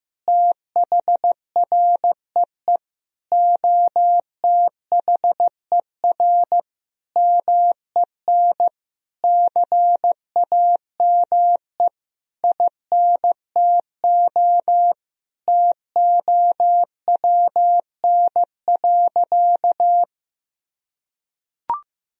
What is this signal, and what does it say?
Morse code, heard as a keyed tone, about 15 wpm, 700 Hz. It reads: THREE OTHER MEN CAME INTO TOWN.
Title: Sentences from Top 1000 Words - Review Part 2 - 15wpm